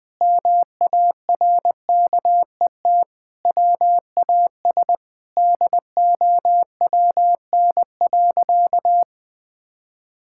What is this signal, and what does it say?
Morse code, heard as a keyed tone, about 20 wpm, 700 Hz. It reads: MARKET WAS DOWN.
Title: Sentences with 5-Word Phrases - 20wpm